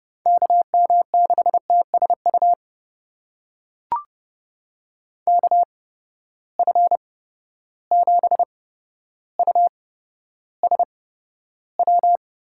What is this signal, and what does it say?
Morse code, heard as a keyed tone, about 30 wpm, spaced as for 13 wpm, 700 Hz. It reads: KM6TSU KF7USW
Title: USA Call Signs - 6 Characters - 8x Character Spacing - 30wpm